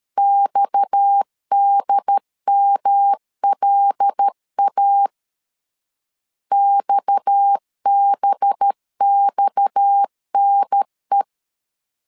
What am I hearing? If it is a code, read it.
Morse code, heard as a keyed tone, about 13 wpm, 800 Hz. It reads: XDMLA XBXNE
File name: LEZIONI / 6 PKX0 / 6 PKX 12wpm 2024